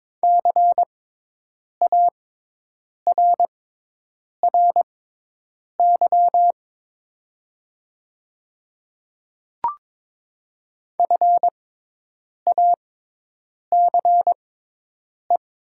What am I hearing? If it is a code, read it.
Morse code, heard as a keyed tone, about 22 wpm, spaced as for 8 wpm, 700 Hz. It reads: CARRY FACE
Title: 6x Character Spacing - Top 300 Words - 22wpm